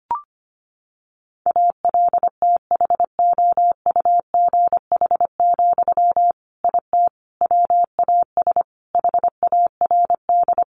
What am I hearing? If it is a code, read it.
Morse code, heard as a keyed tone, about 25 wpm, 700 Hz. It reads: ALTHOUGH, IT WAS HARD
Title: Sentences from Top 600 Words 25wpm